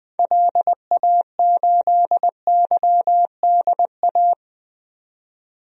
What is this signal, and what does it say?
Morse code, heard as a keyed tone, about 21 wpm, 700 Hz. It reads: LA8YDA
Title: DXCC-100 Call Signs - 6 Characters - 20wpm